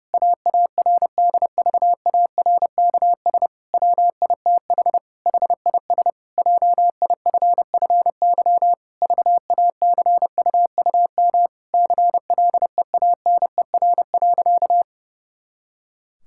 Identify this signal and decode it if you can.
Morse code, heard as a keyed tone, about 30 wpm, 700 Hz. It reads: AARDVARKS WITH HIS JIFFY VACUUM CLEANER.
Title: Pangram Challenge v2 - 30wpm